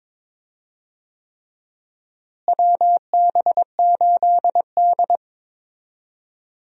Morse code, heard as a keyed tone, about 21 wpm, 700 Hz. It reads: WB8D
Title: USA Call Signs - 4 Characters - Repeated 3x - 22wpm